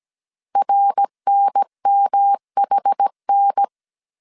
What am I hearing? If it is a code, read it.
Morse code, heard as a keyed tone, about 17 wpm, 800 Hz. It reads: RNMHN